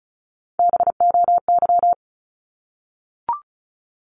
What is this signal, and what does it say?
Morse code, heard as a keyed tone, about 34 wpm, 700 Hz. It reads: BOY
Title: Top 200 Words 35wpm